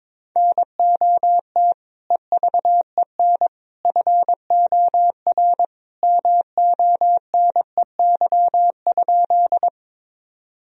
Morse code, heard as a keyed tone, about 22 wpm, 700 Hz. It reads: NOT EVEN FOR MONEY?
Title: Sentences from Top 1000 Words - Review Part 1 - 22wpm